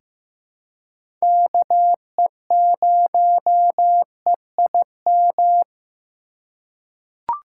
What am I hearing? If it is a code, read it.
Morse code, heard as a keyed tone, about 15 wpm, 700 Hz. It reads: KE0EIM
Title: USA Call Signs - 6 Characters - 4x Character Spacing - 15wpm